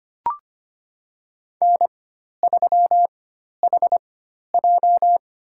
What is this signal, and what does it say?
Morse code, heard as a keyed tone, about 25 wpm, spaced as for 12 wpm, 700 Hz. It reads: N3HJ
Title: USA Call Signs - 4x Character Spacing - 25wpm